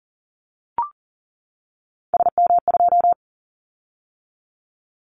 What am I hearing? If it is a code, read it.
Morse code, heard as a keyed tone, about 39 wpm, 700 Hz. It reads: SM2
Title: Callsign Prefixes for Top 25 DXCC - 40wpm